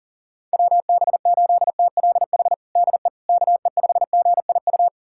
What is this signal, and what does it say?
Morse code, heard as a keyed tone, about 40 wpm, 700 Hz. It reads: WB8TLH DE KE5GIU
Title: USA Callsign DE Callsign - 6 Characters - 40wpm